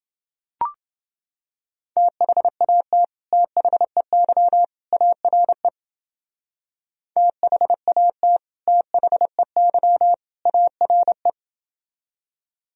Morse code, heard as a keyed tone, about 29 wpm, 700 Hz. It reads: THAT THEY ARE THAT THEY ARE